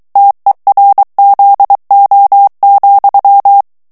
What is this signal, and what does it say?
Morse code, heard as a keyed tone, about 23 wpm, 800 Hz. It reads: TERZO,
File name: ITALIEN texte clair 2 vitesse 22 mots minute